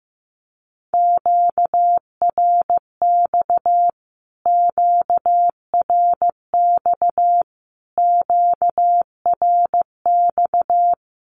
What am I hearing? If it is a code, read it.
Morse code, heard as a keyed tone, about 15 wpm, 700 Hz. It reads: QRX QRX QRX
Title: Q-Codes Repeated 3x 15wpm